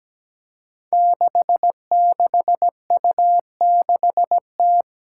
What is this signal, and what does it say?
Morse code, heard as a keyed tone, about 17 wpm, 700 Hz. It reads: 66U6T
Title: Five Challenging Letter-Number Combinations 17wpm